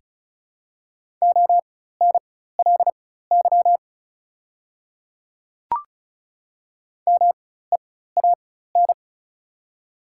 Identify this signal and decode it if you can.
Morse code, heard as a keyed tone, about 36 wpm, spaced as for 16 wpm, 690 Hz. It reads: ONLY MEAN
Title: 4x Character Spacing - Top 500 Words - 35wpm